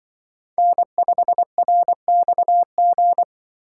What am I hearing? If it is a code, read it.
Morse code, heard as a keyed tone, about 24 wpm, 700 Hz. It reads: N5RXG
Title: Speed Racing - USA 1x3 Call Signs 20wpm